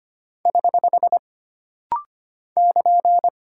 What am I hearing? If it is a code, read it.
Morse code, heard as a keyed tone, about 25 wpm, 700 Hz. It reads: <HH> <KN>
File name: Single Punctuation-Prosign - Rapid-Fire - 25wpm